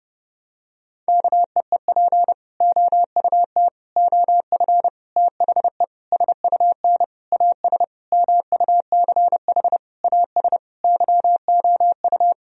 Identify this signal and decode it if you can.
Morse code, heard as a keyed tone, about 30 wpm, 700 Hz. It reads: KEEP OUT OF THE SUN AS MUCH AS YOU
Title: Sentences from Top 1000 Words - Review Part 1 - 30wpm